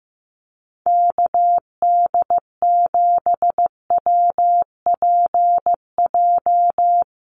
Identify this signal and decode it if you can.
Morse code, heard as a keyed tone, about 15 wpm, 700 Hz. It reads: KD7WPJ
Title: NA SOTA Activator Callsigns 15wpm